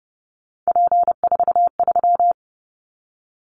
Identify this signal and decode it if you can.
Morse code, heard as a keyed tone, about 30 wpm, 700 Hz. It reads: P43
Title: Callsign Prefixes for Top 100 DXCC - 30wpm